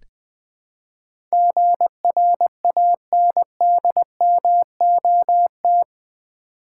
Morse code, heard as a keyed tone, about 20 wpm, 700 Hz. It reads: GRANDMOT
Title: Build A Word - 4 Letters At A Time - 11-Letter Words - 20wpm